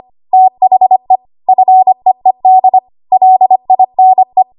Morse code, heard as a keyed tone, about 25 wpm, 750 Hz. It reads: THE FEED LINE
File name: Archive / 25 WPM / 25WPM